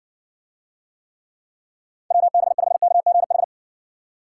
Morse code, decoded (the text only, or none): WB5KGH